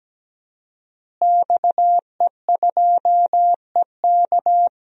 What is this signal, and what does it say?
Morse code, heard as a keyed tone, about 17 wpm, 700 Hz. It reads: XE2EK
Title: DXCC-100 Call Signs - 5 Characters - Repeated 3x - 17wpm